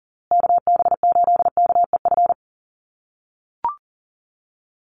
Morse code, heard as a keyed tone, about 40 wpm, 700 Hz. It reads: KB8KEF
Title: USA Call Signs Repeated 3x 40wpm